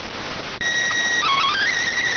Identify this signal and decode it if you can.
SSTV (slow-scan television), PD 290.